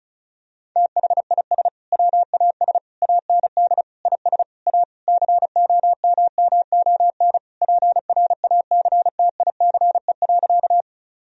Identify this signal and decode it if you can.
Morse code, heard as a keyed tone, about 36 wpm, 700 Hz. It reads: THIS WAS AND IS A COMMON PRACTICE.
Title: Sentences from Top 1000 Words - Review Part 4 - 35wpm